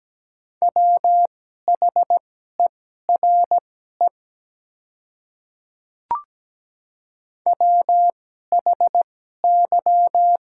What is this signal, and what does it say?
Morse code, heard as a keyed tone, about 17 wpm, spaced as for 12 wpm, 700 Hz. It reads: WHERE WHY
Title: 2x Character Spacing - Top 300 Words - 17wpm